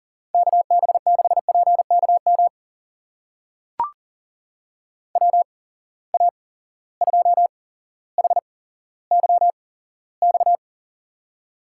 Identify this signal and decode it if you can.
Morse code, heard as a keyed tone, about 39 wpm, spaced as for 17 wpm, 700 Hz. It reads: KB6PKM WA2HYX
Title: USA Call Signs - 6 Characters - 8x Character Spacing - 40wpm